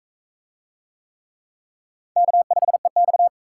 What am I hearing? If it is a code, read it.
Morse code, heard as a keyed tone, about 41 wpm, 700 Hz. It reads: K5EX